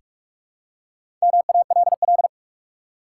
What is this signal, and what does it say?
Morse code, heard as a keyed tone, about 44 wpm, 700 Hz. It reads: MALL
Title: Four Letter Words - 45wpm